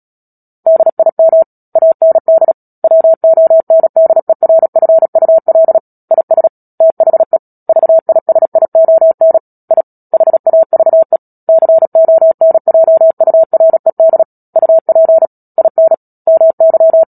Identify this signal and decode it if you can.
Morse code, heard as a keyed tone, about 37 wpm, 650 Hz. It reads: DIM AND WONDERFUL IS THE VISION I HAVE CONJURED UP IN MY
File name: Book Two / 10-Chapter 10 - Epilogue - 36 WPM